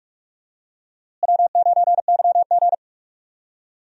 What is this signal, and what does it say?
Morse code, heard as a keyed tone, about 42 wpm, 700 Hz. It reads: W9YG
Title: USA Call Signs - 4 Characters - Repeated 2x - 45wpm